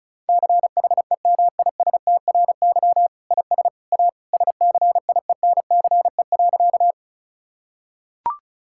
Morse code, heard as a keyed tone, about 36 wpm, 700 Hz. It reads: CHEMISTRY IS A SCIENCE.
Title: Sentences from Top 1000 Words - Review Part 6 - 35wpm